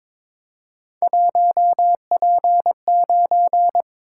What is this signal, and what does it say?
Morse code, heard as a keyed tone, about 22 wpm, 700 Hz. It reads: 1P9
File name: practice / 22wpm / Three Letter-Number Combinations - 22wpm